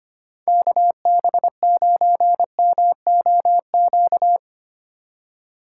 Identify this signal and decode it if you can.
Morse code, heard as a keyed tone, about 25 wpm, 700 Hz. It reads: KB9MOQ
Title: USA Call Signs - 6 Characters - 2x Character Spacing - 25wpm